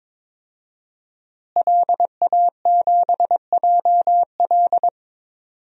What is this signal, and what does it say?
Morse code, heard as a keyed tone, about 22 wpm, 700 Hz. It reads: LA7JL
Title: DXCC-100 Call Signs - 5 Characters - Repeated 3x - 22wpm